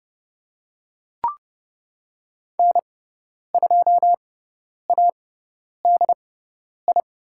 Morse code, heard as a keyed tone, about 30 wpm, spaced as for 10 wpm, 700 Hz. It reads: N2ADI
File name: Lesson 17g - Letter u - Call Signs 30wpm